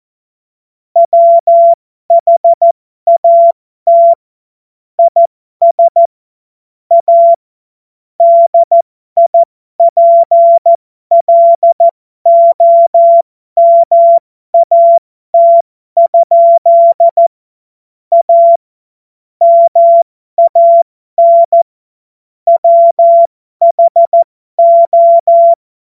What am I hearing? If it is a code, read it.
Morse code, heard as a keyed tone, about 14 wpm, spaced as for 12 wpm, 680 Hz. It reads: WHAT IS A DIPLOMAT? A MAN WHO
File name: Morsekurs / Witze